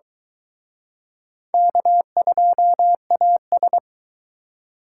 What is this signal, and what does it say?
Morse code, heard as a keyed tone, about 23 wpm, 700 Hz. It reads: K2AS